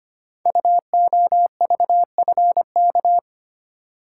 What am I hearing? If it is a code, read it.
Morse code, heard as a keyed tone, about 25 wpm, 700 Hz. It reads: UOVFK
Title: Code Groups - 5 Letters - 25wpm